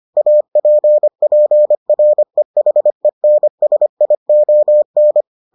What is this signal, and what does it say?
Morse code, heard as a keyed tone, about 25 wpm, 600 Hz. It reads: APPREHENSION